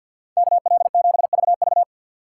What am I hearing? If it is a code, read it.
Morse code, heard as a keyed tone, about 52 wpm, 700 Hz. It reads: KL7VV
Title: USA Call Signs - 5 Characters - 4x Character Spacing - 50wpm